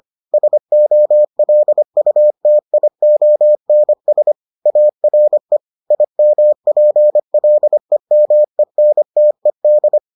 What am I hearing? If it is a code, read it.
Morse code, heard as a keyed tone, about 25 wpm, 600 Hz. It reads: SOLUTIONS ARE IMPLEMENTED